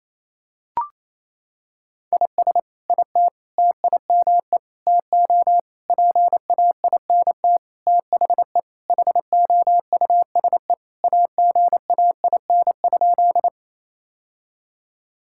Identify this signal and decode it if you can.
Morse code, heard as a keyed tone, about 27 wpm, 700 Hz. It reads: IS IT TIME TO PAINT THE HOUSE AGAIN?
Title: Sentences from Top 1000 Words - Review Part 6 - 28wpm